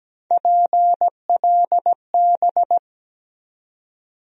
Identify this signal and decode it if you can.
Morse code, heard as a keyed tone, about 17 wpm, 700 Hz. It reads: PLB
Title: USA Call Sign Postfixes - 3 Letters 17wpm